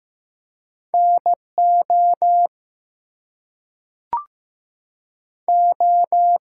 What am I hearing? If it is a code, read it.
Morse code, heard as a keyed tone, about 15 wpm, 700 Hz. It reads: NO O
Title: Two Letter Words - 15wpm